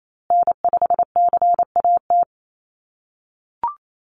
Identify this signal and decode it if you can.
Morse code, heard as a keyed tone, about 28 wpm, 700 Hz. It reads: N5CAT
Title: USA Call Signs Repeated 2x 28wpm